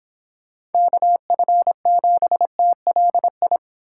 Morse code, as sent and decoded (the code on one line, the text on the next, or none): -.- ..-. --... - .-.. ..
KF7TLI